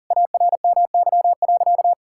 Morse code, heard as a keyed tone, about 40 wpm, 700 Hz. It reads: ARMY.
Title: Sentences from Top 1000 Words - Review Part 5 - 40wpm